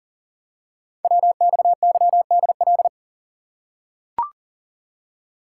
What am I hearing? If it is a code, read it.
Morse code, heard as a keyed tone, about 38 wpm, 700 Hz. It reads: WXYDL
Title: Code Groups - 5 Letters - 40wpm